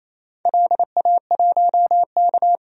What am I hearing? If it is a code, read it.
Morse code, heard as a keyed tone, about 27 wpm, 700 Hz. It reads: LA1K
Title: DXCC-100 Call Signs - 4 Characters - 28wpm